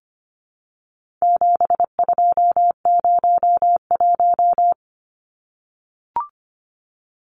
Morse code, heard as a keyed tone, about 25 wpm, 700 Hz. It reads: 7201
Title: Numbers - 4-Digit 25wpm